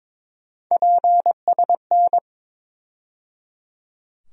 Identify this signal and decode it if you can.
Morse code, heard as a keyed tone, about 21 wpm, 700 Hz. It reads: PSN
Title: USA Call Sign Postfixes - 3 Letters 22wpm